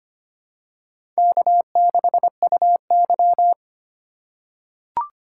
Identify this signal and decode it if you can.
Morse code, heard as a keyed tone, about 25 wpm, 700 Hz. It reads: K6UY
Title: USA Call Signs - 4 Characters - 3x Character Spacing - 25wpm